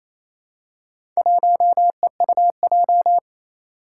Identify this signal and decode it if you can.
Morse code, heard as a keyed tone, about 28 wpm, 700 Hz. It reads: 1EUJ
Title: Four Challenging Letter-Number Combinations 28wpm